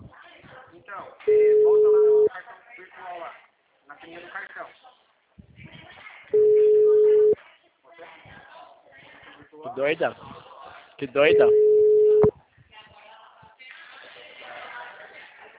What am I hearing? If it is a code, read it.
Ringback tone (Europe, 425 Hz), with speech.
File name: out- -2007- - - .93056